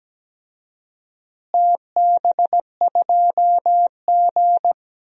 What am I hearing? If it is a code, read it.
Morse code, heard as a keyed tone, about 17 wpm, 700 Hz. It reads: TB2G
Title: DXCC-100 Call Signs - 4 Characters - Repeated 3x - 17wpm